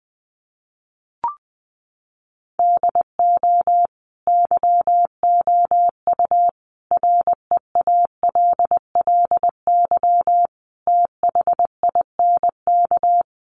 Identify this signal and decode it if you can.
Morse code, heard as a keyed tone, about 20 wpm, 700 Hz. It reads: DO YOU REALLY THINK
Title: Sentences from Top 600 Words 20wpm